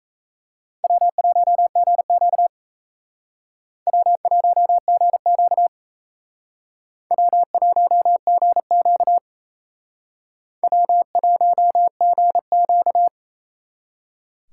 Morse code, heard as a keyed tone, about 36 wpm, 700 Hz. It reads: W1GQ W1GQ W1GQ W1GQ